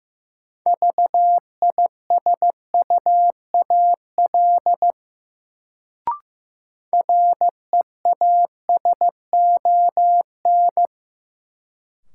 Morse code, heard as a keyed tone, about 15 wpm, 700 Hz. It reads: VISUAL REASON
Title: Sets of 5 Words Encore - New Word Drill - 15wpm